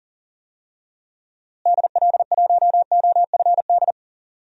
Morse code, heard as a keyed tone, about 40 wpm, 700 Hz. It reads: DL1OFD
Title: DXCC-100 Call Signs - 6 Characters - Repeated 3x - 40wpm